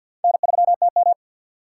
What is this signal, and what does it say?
Morse code, heard as a keyed tone, about 53 wpm, 700 Hz. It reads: N3TM
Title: USA Call Signs - 4 Characters - 2x Character Spacing - 50wpm